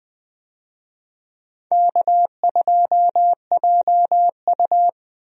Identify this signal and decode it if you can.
Morse code, heard as a keyed tone, about 20 wpm, 700 Hz. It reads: K2JU
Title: USA Call Signs - 4 Characters - Repeated 2x - 20wpm